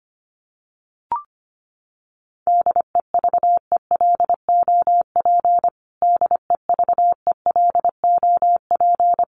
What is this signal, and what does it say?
Morse code, heard as a keyed tone, about 25 wpm, 700 Hz. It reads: DEVELOP DEVELOP
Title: Top 500 Words Repeated 25wpm